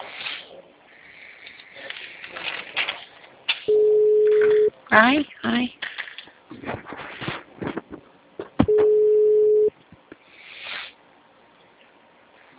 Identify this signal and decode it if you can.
Ringback tone (Europe, 425 Hz), with speech.
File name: out- -2003- - - .23